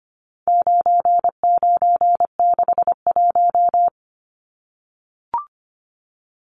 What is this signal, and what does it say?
Morse code, heard as a keyed tone, about 25 wpm, 700 Hz. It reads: 9961
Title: Numbers - 4-Digit 25wpm